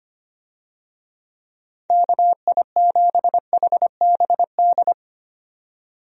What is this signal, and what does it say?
Morse code, heard as a keyed tone, about 25 wpm, 700 Hz. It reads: KI7HBD
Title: USA Call Signs - 6 Characters - 25wpm